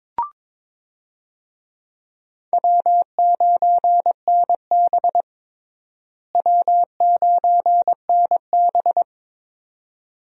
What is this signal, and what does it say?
Morse code, heard as a keyed tone, about 21 wpm, 700 Hz. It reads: W9NB W9NB
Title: USA Call Signs - 4 Characters - Repeated 2x - 22wpm